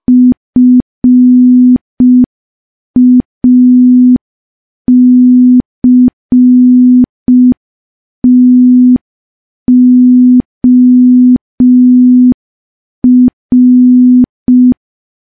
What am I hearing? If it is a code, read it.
Morse code, heard as a keyed tone, about 5 wpm, 250 Hz. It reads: FACTOR
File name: morse / morse-training-05WPM